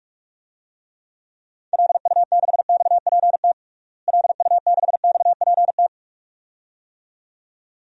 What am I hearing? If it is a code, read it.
Morse code, heard as a keyed tone, about 46 wpm, 700 Hz. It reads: LU6XPT LU6XPT
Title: DXCC-100 Call Signs - 6 Characters - Repeated 2x - 45wpm